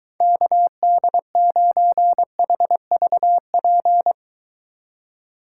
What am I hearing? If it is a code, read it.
Morse code, heard as a keyed tone, about 23 wpm, 700 Hz. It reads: KD9HVP